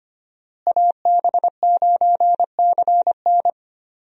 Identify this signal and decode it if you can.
Morse code, heard as a keyed tone, about 25 wpm, 700 Hz. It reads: AB9CN
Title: USA Call Signs - 5 Characters - Repeated 3x - 25wpm